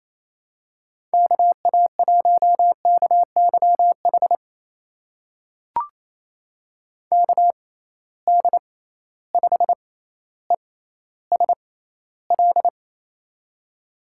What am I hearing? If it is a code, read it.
Morse code, heard as a keyed tone, about 28 wpm, spaced as for 14 wpm, 700 Hz. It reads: KA1KYH KD5ESL
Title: USA Call Signs - 6 Characters - 6x Character Spacing - 28wpm